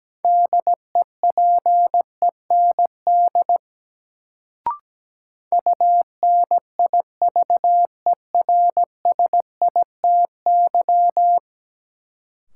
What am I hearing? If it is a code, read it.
Morse code, heard as a keyed tone, about 17 wpm, 700 Hz. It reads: DEPEND UNIVERSITY
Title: Sentences from Top 1300 Words - New Word Drill Alt - 17wpm